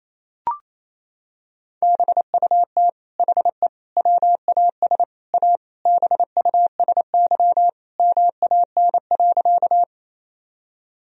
Morse code, heard as a keyed tone, about 27 wpm, 690 Hz. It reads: BUT HE WAS A BUSY MAN.
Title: Sentences from Top 1000 Words - Review Part 3 - 28wpm